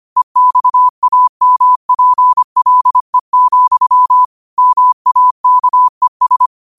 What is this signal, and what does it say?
Morse code, heard as a keyed tone, about 25 wpm, 1000 Hz. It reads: EXAMPLE, MAKES
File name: qotdfiles / 25wpm